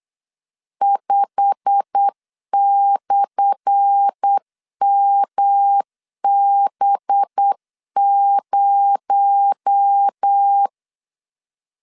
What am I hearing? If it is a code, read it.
Morse code, heard as a keyed tone, about 8 wpm, 800 Hz. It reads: 5/MB0
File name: LEZIONI / 10 SIGNS / 10A SLASH 8wpm 3452